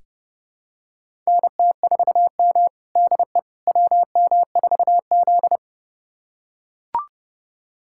Morse code, heard as a keyed tone, about 30 wpm, 700 Hz. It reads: NT4M DE WM4Z
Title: USA Callsign DE Callsign - 4 Characters - 30wpm